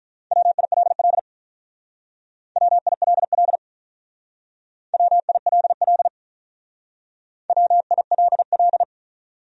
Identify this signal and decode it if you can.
Morse code, heard as a keyed tone, about 43 wpm, 700 Hz. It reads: WILL WILL WILL WILL
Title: Speed-Racing Top 100 Words 35wpm